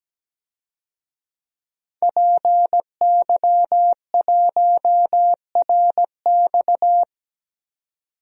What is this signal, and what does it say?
Morse code, heard as a keyed tone, about 17 wpm, 700 Hz. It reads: PY1RX
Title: DXCC-100 Call Signs - 5 Characters - 17wpm